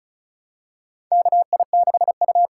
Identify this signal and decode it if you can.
Morse code, heard as a keyed tone, about 36 wpm, 700 Hz. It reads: KI6U
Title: USA Call Signs - 6 Characters - 8x Character Spacing - 35wpm